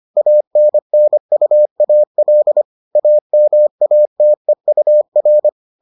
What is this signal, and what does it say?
Morse code, heard as a keyed tone, about 25 wpm, 600 Hz. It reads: ANNUAL AMATEUR